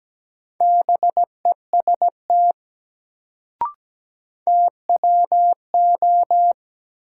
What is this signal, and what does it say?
Morse code, heard as a keyed tone, about 17 wpm, 700 Hz. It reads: BEST TWO
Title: Sets of 2 Words - New Word Drill - 17wpm